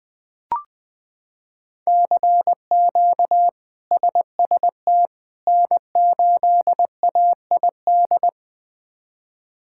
Morse code, heard as a keyed tone, about 20 wpm, 700 Hz. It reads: CQ SST N8AID
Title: K1USN SST QSO Practice - 20wpm